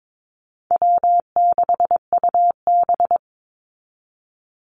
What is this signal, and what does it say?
Morse code, heard as a keyed tone, about 22 wpm, 700 Hz. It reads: W6UB
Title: NA SOTA Activator Callsigns 22wpm